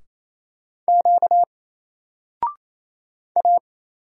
Morse code, heard as a keyed tone, about 28 wpm, 700 Hz. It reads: Q A
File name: Single-Letter-Number-28wpm